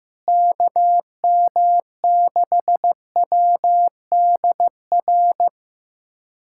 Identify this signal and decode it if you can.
Morse code, heard as a keyed tone, about 15 wpm, 700 Hz. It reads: KM6WDR
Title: USA Callsign DE Callsign - 6 Characters - 15wpm